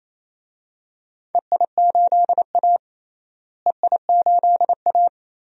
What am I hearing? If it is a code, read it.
Morse code, heard as a keyed tone, about 28 wpm, 700 Hz. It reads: EI8A EI8A
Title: DXCC-100 Call Signs - 4 Characters - Repeated 2x - 28wpm